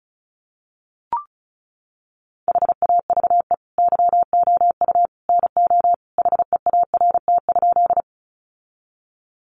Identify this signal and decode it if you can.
Morse code, heard as a keyed tone, about 34 wpm, 700 Hz. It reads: HAVE YOU NO HEART?
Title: Sentences from Top 1200 Words 35wpm